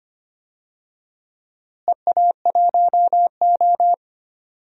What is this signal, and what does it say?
Morse code, heard as a keyed tone, about 25 wpm, 700 Hz. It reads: EA1O
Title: DXCC-100 Call Signs - 4 Characters - Repeated 2x - 25wpm